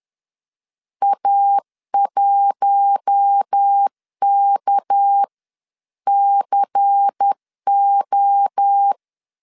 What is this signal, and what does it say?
Morse code, heard as a keyed tone, about 10 wpm, spaced as for 9 wpm, 800 Hz. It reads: A1KCO